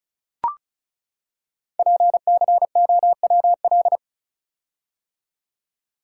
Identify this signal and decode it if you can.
Morse code, heard as a keyed tone, about 36 wpm, 690 Hz. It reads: PCOWL
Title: Code Groups - 5 Letters - 35wpm